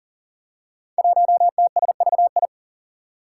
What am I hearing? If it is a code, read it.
Morse code, heard as a keyed tone, about 42 wpm, 700 Hz. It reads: 1TSVI
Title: Five Challenging Letter-Number Combinations 40wpm